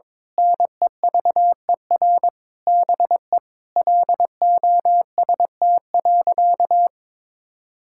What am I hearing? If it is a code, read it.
Morse code, heard as a keyed tone, about 22 wpm, 700 Hz. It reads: NEVER BE LOST.
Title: Sentences from Top 1600 Words 22wpm